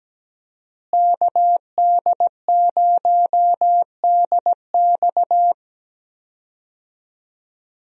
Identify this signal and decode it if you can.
Morse code, heard as a keyed tone, about 17 wpm, 700 Hz. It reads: KD0DX
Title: USA Call Signs - 5 Characters - Repeated 2x - 17wpm